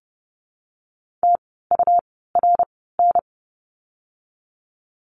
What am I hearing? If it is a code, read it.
Morse code, heard as a keyed tone, about 30 wpm, spaced as for 17 wpm, 700 Hz. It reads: TURN
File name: Lesson 17f - Letter u - Words 30wpm